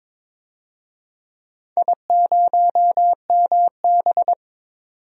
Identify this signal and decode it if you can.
Morse code, heard as a keyed tone, about 22 wpm, 700 Hz. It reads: I0MB
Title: DXCC-100 Call Signs - 4 Characters - 22wpm